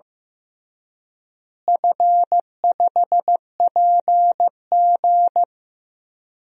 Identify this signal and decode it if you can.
Morse code, heard as a keyed tone, about 15 wpm, 700 Hz. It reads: F5PG
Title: DXCC-100 Call Signs - 4 Characters - Repeated 3x - 15wpm